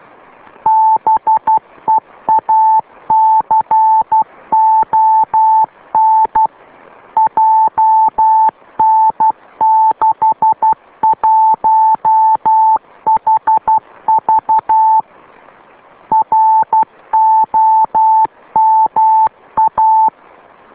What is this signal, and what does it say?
Morse code, heard as a keyed tone, about 12 wpm, 880 Hz. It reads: BEACON JN61HV ROMA